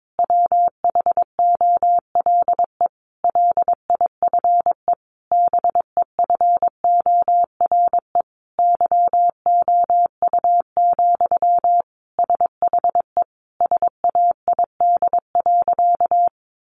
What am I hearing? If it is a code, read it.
Morse code, heard as a keyed tone, about 22 wpm, 700 Hz. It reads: WHOLE LIFE BEFORE YOU, SHE SAID.
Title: Sentences from Top 400 Words 22wpm